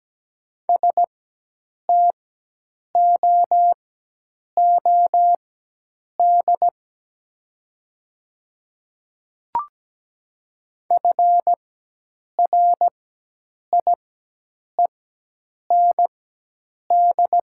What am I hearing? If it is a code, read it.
Morse code, heard as a keyed tone, about 17 wpm, spaced as for 8 wpm, 690 Hz. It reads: STOOD FRIEND
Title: 4x Character Spacing - Top 500 Words - 17wpm